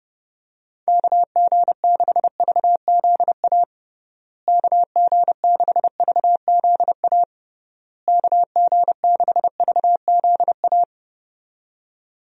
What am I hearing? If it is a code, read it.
Morse code, heard as a keyed tone, about 30 wpm, 700 Hz. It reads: KG6VZA KG6VZA KG6VZA
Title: USA Call Signs - 6 Characters - Repeated 3x - 30wpm